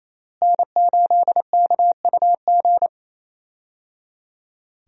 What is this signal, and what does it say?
Morse code, heard as a keyed tone, about 28 wpm, 700 Hz. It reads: N8KUG
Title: USA Call Signs - 5 Characters - Repeated 3x - 28wpm